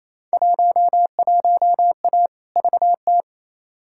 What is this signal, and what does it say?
Morse code, heard as a keyed tone, about 28 wpm, 700 Hz. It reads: 11A VT
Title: Field Day Exchanges Encore - 28wpm